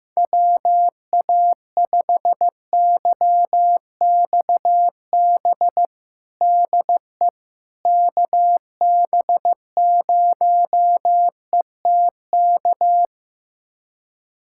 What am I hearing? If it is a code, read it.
Morse code, heard as a keyed tone, about 16 wpm, 700 Hz. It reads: WA5YXB DE KB0ETK
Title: USA Callsign DE Callsign - 6 Characters - 15wpm